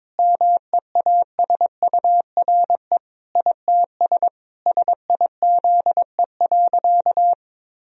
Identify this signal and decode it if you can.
Morse code, heard as a keyed tone, about 22 wpm, 700 Hz. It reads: MEASURE ITS SIZE.
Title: Sentences from Top 1000 Words - Review Part 2 - 22wpm